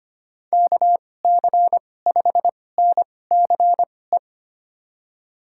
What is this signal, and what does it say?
Morse code, heard as a keyed tone, about 25 wpm, spaced as for 18 wpm, 700 Hz. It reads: KC5NCE
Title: USA Call Signs - 6 Characters - 2x Character Spacing - 25wpm